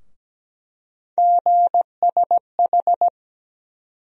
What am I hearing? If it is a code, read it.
Morse code, heard as a keyed tone, about 17 wpm, 700 Hz. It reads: GSH
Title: USA Call Sign Postfixes - 3 Letters 17wpm